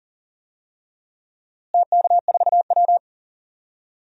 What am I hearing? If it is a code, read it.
Morse code, heard as a keyed tone, about 40 wpm, 700 Hz. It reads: TK4W
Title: DXCC-100 Call Signs - 4 Characters - Repeated 3x - 40wpm